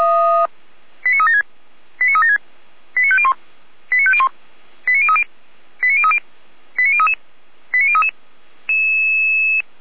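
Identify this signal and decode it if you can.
Five-tone selective call (ZVEI), heard as quick runs of five tones, tones of about 70 ms. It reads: 89376 89376 89501 89501 89030 89030 89033 89033
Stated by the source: ZVEI five-tone call received on an SDR by Feuerwehr Orlamünde.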